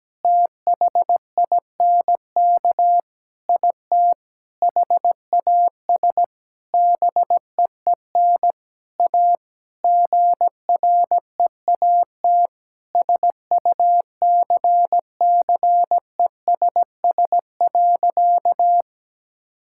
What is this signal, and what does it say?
Morse code, heard as a keyed tone, about 17 wpm, 700 Hz. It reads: THINK IT HAS BEEN A GREAT SUCCESS.
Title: Sentences from Top 1000 Words - Review Part 6 - 17wpm